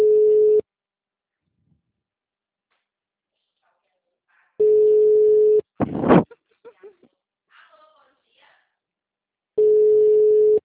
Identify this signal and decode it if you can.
Ringback tone (Europe, 425 Hz), with speech.
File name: out- -2019- - - .21481